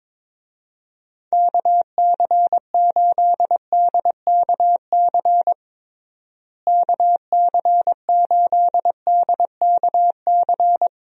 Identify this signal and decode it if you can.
Morse code, heard as a keyed tone, about 22 wpm, 700 Hz. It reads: KC8DKC KC8DKC
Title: USA Call Signs - 6 Characters - Repeated 3x - 22wpm